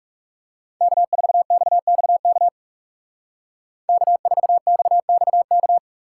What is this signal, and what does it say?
Morse code, heard as a keyed tone, about 39 wpm, 700 Hz. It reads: K4XXK K4XXK